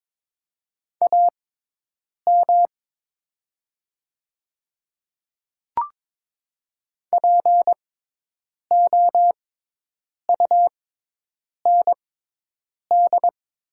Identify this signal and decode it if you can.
Morse code, heard as a keyed tone, about 22 wpm, spaced as for 8 wpm, 700 Hz. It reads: AM POUND
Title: 6x Character Spacing - Top 500 Words - 22wpm